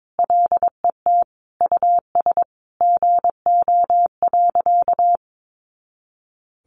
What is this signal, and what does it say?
Morse code, heard as a keyed tone, about 22 wpm, 700 Hz. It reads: LET US GO.
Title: Sentences from Top 400 Words 22wpm